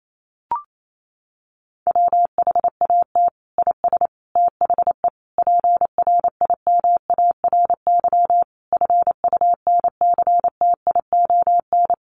Morse code, heard as a keyed tone, about 28 wpm, 700 Hz. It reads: WHAT IS THE PRIMARY FUNCTION
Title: Sentences from Top 1100 Words 28wpm